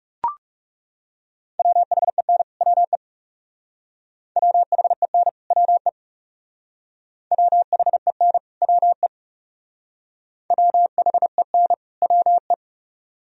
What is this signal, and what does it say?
Morse code, heard as a keyed tone, about 38 wpm, 700 Hz. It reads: WHEN WE WHEN WE WHEN WE WHEN WE